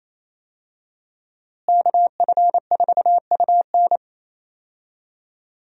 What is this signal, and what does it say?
Morse code, heard as a keyed tone, about 27 wpm, 700 Hz. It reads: KF4UN
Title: USA Call Signs - 5 Characters - Repeated 3x - 28wpm